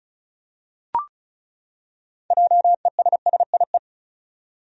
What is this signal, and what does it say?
Morse code, heard as a keyed tone, about 34 wpm, 700 Hz. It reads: JESSIE
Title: USA Common Names - 35wpm